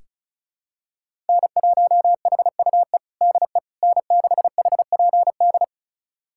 Morse code, heard as a keyed tone, about 36 wpm, 700 Hz. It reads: N1HUE DE N6HPD
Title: USA Callsign DE Callsign 35wpm